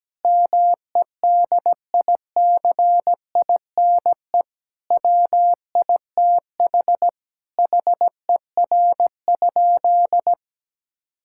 Morse code, heard as a keyed tone, about 17 wpm, 700 Hz. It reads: MEDICINE WITH HER?
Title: Sentences from Top 1000 Words - Review Part 6 - 17wpm